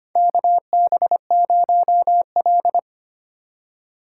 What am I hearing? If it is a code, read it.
Morse code, heard as a keyed tone, about 25 wpm, 700 Hz. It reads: KB0L
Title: USA Call Signs - 4 Characters - 4x Character Spacing - 25wpm